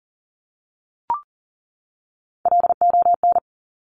Morse code, heard as a keyed tone, about 38 wpm, 690 Hz. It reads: LON
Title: Top Three Letter Combinations in Words 40wpm